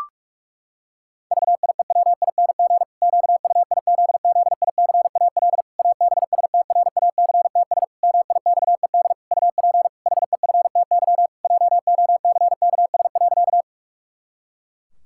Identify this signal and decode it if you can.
Morse code, heard as a keyed tone, about 41 wpm, 700 Hz. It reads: VIEWING QUIZZICAL ABSTRACTS MIXED UP HEFTY JOCKS.
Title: Pangram Challenge v2 - 45wpm